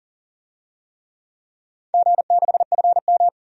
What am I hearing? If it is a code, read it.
Morse code, heard as a keyed tone, about 40 wpm, 700 Hz. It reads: G6FM
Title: DXCC-100 Call Signs - 4 Characters - 40wpm